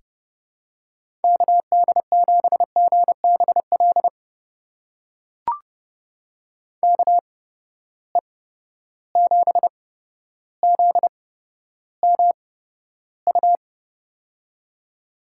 Morse code, heard as a keyed tone, about 30 wpm, spaced as for 13 wpm, 700 Hz. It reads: KD7GBL KE7ZMU
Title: USA Call Signs - 6 Characters - 8x Character Spacing - 30wpm